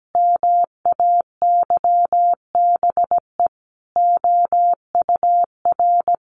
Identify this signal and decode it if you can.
Morse code, heard as a keyed tone, about 17 wpm, 700 Hz. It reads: MAYBE OUR
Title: A Sound Of Thunder 17wpm